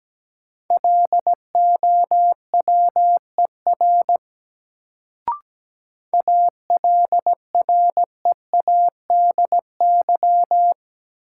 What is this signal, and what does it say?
Morse code, heard as a keyed tone, about 17 wpm, 700 Hz. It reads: LOWER ALREADY
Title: Sentences from Top 700 Words - New Word Drill - 17wpm